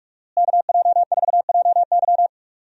Morse code, heard as a keyed tone, about 46 wpm, 700 Hz. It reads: KJ4JY
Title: USA Call Signs - 5 Characters - Repeated 2x - 45wpm